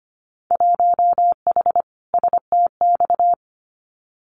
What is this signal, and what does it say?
Morse code, heard as a keyed tone, about 25 wpm, 700 Hz. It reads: 1H STX
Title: Winter Field Day Exchanges Encore 25wpm